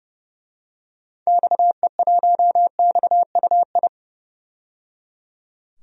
Morse code, heard as a keyed tone, about 30 wpm, 700 Hz. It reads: XE1XUI
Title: DXCC-100 Call Signs - 6 Characters - 30wpm